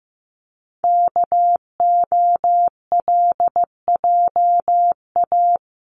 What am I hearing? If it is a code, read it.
Morse code, heard as a keyed tone, about 15 wpm, 700 Hz. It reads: KOLJA
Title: CWT Exchange 15wpm